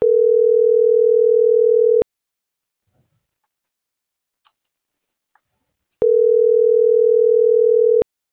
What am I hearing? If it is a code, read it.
Ringback tone (North America), 440 + 480 Hz.